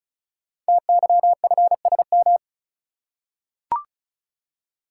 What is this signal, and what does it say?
Morse code, heard as a keyed tone, about 36 wpm, 700 Hz. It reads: TYFSM
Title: Code Groups - 5 Letters - 35wpm